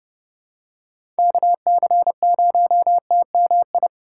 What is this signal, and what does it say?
Morse code, heard as a keyed tone, about 30 wpm, 700 Hz. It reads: KC0TMI
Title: USA Call Signs - 6 Characters - 3x Character Spacing - 30wpm